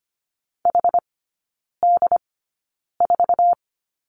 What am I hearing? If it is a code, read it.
Morse code, heard as a keyed tone, about 25 wpm, spaced as for 9 wpm, 700 Hz. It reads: HD4
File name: Lesson 17e - Letter u - Three Characters 25wpm